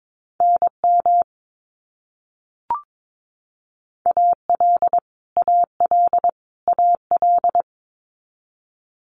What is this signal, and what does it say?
Morse code, heard as a keyed tone, about 22 wpm, 700 Hz. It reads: NM AL AL AL
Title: US States Repeated 3x 22wpm